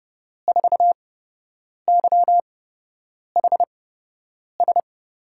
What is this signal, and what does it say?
Morse code, heard as a keyed tone, about 30 wpm, spaced as for 8 wpm, 700 Hz. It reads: 4YHS
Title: USA Call Signs - 5 Characters - 8x Character Spacing - 30wpm